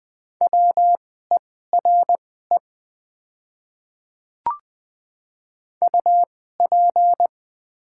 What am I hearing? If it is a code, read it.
Morse code, heard as a keyed tone, about 20 wpm, spaced as for 14 wpm, 700 Hz. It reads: WERE UP
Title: 2x Character Spacing - Top 100 Words - 20wpm